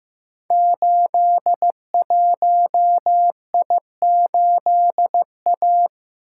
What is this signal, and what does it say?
Morse code, heard as a keyed tone, about 15 wpm, 700 Hz. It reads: 81I8A
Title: Code Groups - 5 Letter:Numbers - 15wpm